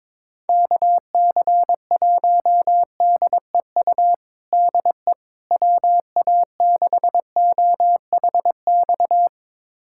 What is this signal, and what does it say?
Morse code, heard as a keyed tone, about 22 wpm, 700 Hz. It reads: KC1DEU DE WA6OHX
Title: USA Callsign DE Callsign - 6 Characters - 22wpm